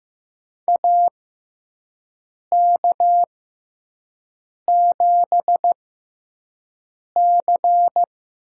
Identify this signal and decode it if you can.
Morse code, heard as a keyed tone, about 15 wpm, spaced as for 5 wpm, 700 Hz. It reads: AK7C
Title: USA Call Signs - 6x Character Spacing - 15wpm